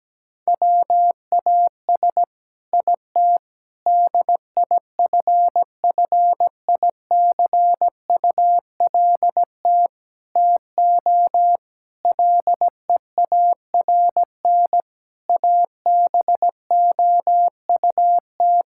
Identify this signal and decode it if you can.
Morse code, heard as a keyed tone, about 17 wpm, 700 Hz. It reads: WAS IT DIFFICULT TO LEARN ABOUT
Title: Sentences from Top 1000 Words - Review Part 6 - 17wpm